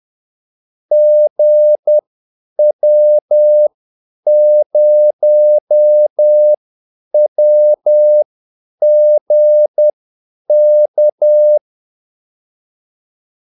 Morse code, heard as a keyed tone, about 10 wpm, spaced as for 8 wpm, 600 Hz. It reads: GW0WGK
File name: Callsigns-10-8wpm